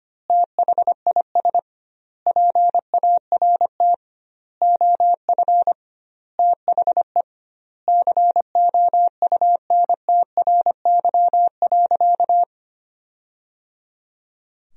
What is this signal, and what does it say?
Morse code, heard as a keyed tone, about 25 wpm, 700 Hz. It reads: THIS PART OF THE COUNTRY.
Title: 2x Word-Spacing - Sentences from Top 300 Words - 25wpm